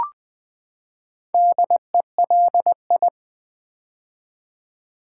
Morse code, heard as a keyed tone, about 20 wpm, 700 Hz. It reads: DELI
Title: Build A Word - 4 Letters At A Time - 12-Letter Words - 20wpm